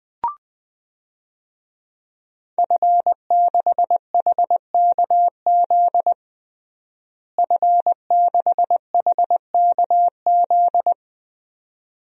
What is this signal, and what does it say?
Morse code, heard as a keyed tone, about 21 wpm, 700 Hz. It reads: F6HKZ F6HKZ
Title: DXCC-100 Call Signs - 5 Characters - Repeated 2x - 20wpm